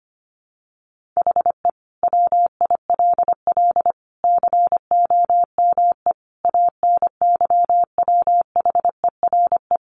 Morse code, heard as a keyed tone, about 25 wpm, 700 Hz. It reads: HE WILL COME ANYWHERE
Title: Sentences from Top 1400 Words 25wpm